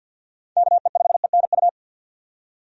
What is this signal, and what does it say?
Morse code, heard as a keyed tone, about 53 wpm, 700 Hz. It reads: KE5ENU
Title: USA Call Signs - 6 Characters - 50wpm